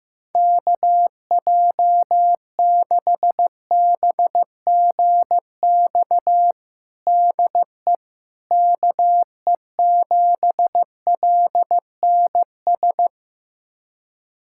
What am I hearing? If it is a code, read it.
Morse code, heard as a keyed tone, about 15 wpm, 700 Hz. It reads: KJ6BGX DE KE7LNS
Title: USA Callsign DE Callsign - 6 Characters - 15wpm